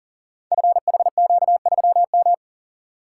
Morse code, heard as a keyed tone, about 39 wpm, 700 Hz. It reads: FHQ3M
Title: Code Groups - 5 Letter:Numbers - 40wpm